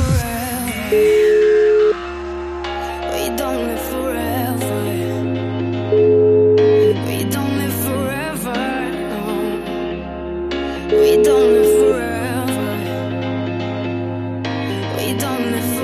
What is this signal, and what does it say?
Ringback tone (Europe, 425 Hz), with noise.